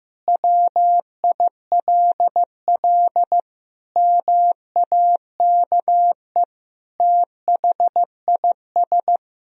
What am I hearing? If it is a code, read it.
Morse code, heard as a keyed tone, about 15 wpm, 700 Hz. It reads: WILL MAKE THIS
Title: Sentences from Top 1000 Words - Review Part 4 - 15wpm